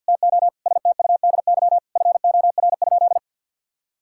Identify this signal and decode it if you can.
Morse code, heard as a keyed tone, about 53 wpm, 700 Hz. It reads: TO STUDY FOR?
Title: Sentences from Top 1000 Words - Review Part 6 - 50wpm